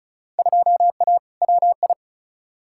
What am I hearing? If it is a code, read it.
Morse code, heard as a keyed tone, about 36 wpm, 700 Hz. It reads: 2A WI
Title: Field Day Exchanges - 35wpm